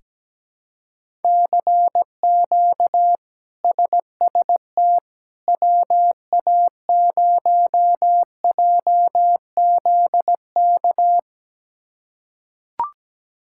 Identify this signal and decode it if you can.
Morse code, heard as a keyed tone, about 17 wpm, 700 Hz. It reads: CQ SST WA0JZK
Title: K1USN SST QSO Practice - 17wpm